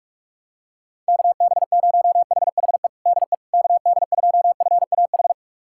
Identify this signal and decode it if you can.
Morse code, heard as a keyed tone, about 41 wpm, 700 Hz. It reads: KB0HHE DE KD2FAH
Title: USA Callsign DE Callsign 45wpm